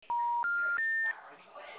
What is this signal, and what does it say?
Special information tone, with speech.
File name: out- -2015- - - .12053